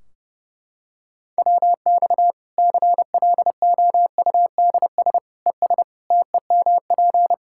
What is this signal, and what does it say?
Morse code, heard as a keyed tone, about 31 wpm, 700 Hz. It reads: WX CLOUDS ES TEMP
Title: Practice QSO - Exchange 2 - Info - 30wpm